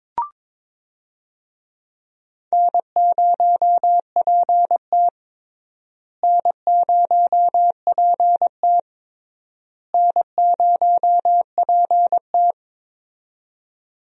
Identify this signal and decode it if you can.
Morse code, heard as a keyed tone, about 22 wpm, 700 Hz. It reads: N0PT N0PT N0PT
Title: USA Call Signs - 4 Characters - Repeated 3x - 22wpm